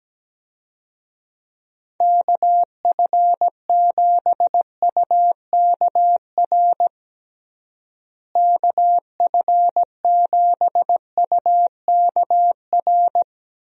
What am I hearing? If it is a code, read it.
Morse code, heard as a keyed tone, about 17 wpm, 700 Hz. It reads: KF7UKR KF7UKR
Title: USA Call Signs - 6 Characters - Repeated 2x - 17wpm